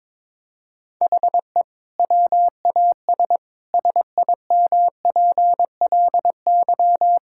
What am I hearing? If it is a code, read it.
Morse code, heard as a keyed tone, about 21 wpm, 700 Hz. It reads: HE WAS SIMPLY
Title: Sentences from Top 1600 Words 22wpm